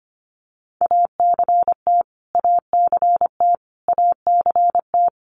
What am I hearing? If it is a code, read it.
Morse code, heard as a keyed tone, about 25 wpm, 700 Hz. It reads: ACT ACT ACT
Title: Top 500 Words Repeated 25wpm